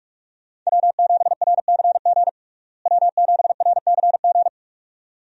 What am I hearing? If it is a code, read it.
Morse code, heard as a keyed tone, about 46 wpm, 700 Hz. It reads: W7RCG W7RCG
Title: USA Call Signs - 5 Characters - Repeated 2x - 45wpm